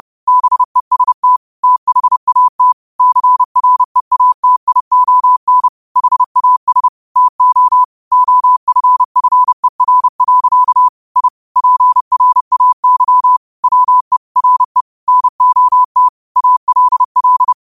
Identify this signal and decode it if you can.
Morse code, heard as a keyed tone, about 30 wpm, 1000 Hz. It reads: BEST THAT CREATION HAS TO OFFER. I PRAY WE RE NOT ALL
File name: qotdfiles / 30wpm